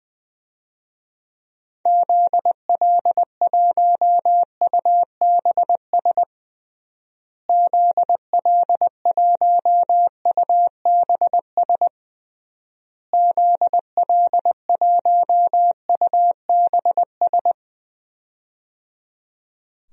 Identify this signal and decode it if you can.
Morse code, heard as a keyed tone, about 20 wpm, 700 Hz. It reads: ZL1UBS ZL1UBS ZL1UBS
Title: DXCC-100 Call Signs - 6 Characters - Repeated 3x - 20wpm